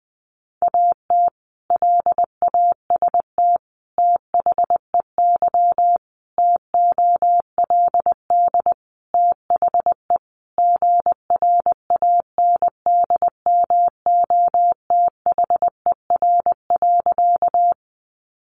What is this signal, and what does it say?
Morse code, heard as a keyed tone, about 20 wpm, 700 Hz. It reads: AT LAST THEY TOLD THE GRANDMOTHER.
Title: Sentences from Top 1200 Words 20wpm